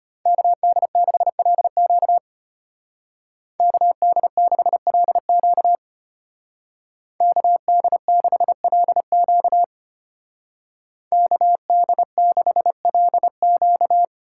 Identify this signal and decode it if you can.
Morse code, heard as a keyed tone, about 31 wpm, 700 Hz. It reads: KD6LQ KD6LQ KD6LQ KD6LQ
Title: Speed Racing - USA 2x2 Call Signs 25wpm